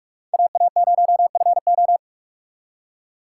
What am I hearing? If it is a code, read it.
Morse code, heard as a keyed tone, about 57 wpm, 700 Hz. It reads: AA0FO